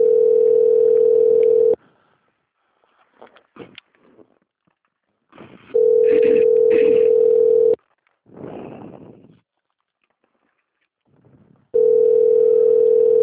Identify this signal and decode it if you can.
Ringback tone (North America), 440 + 480 Hz.
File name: out- - - - - .1094